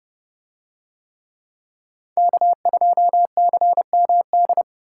Morse code, heard as a keyed tone, about 28 wpm, 700 Hz. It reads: K2CMD